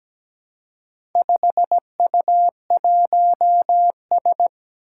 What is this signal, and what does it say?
Morse code, heard as a keyed tone, about 17 wpm, 700 Hz. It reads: 5U1S
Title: Four Challenging Letter-Number Combinations 17wpm